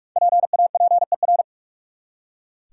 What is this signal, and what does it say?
Morse code, heard as a keyed tone, about 46 wpm, 700 Hz. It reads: PAPER
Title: Five Letter Words - 45wpm